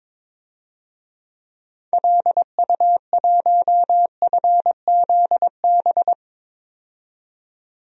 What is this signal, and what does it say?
Morse code, heard as a keyed tone, about 21 wpm, 700 Hz. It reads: LU1FZB
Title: DXCC-100 Call Signs - 6 Characters - 22wpm